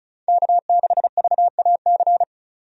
Morse code, heard as a keyed tone, about 36 wpm, 700 Hz. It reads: K6VAC
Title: USA Call Signs - 5 Characters - 35wpm